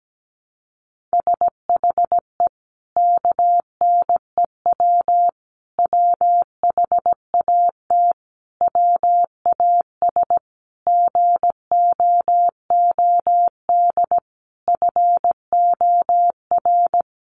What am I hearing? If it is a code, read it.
Morse code, heard as a keyed tone, about 17 wpm, 700 Hz. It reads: SHE KNEW WHAT WAS GOOD FOR
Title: Sentences from Top 400 Words 17wpm